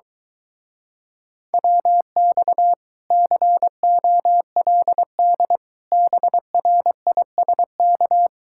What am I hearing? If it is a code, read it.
Morse code, heard as a keyed tone, about 24 wpm, 700 Hz. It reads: WX COLD BRISK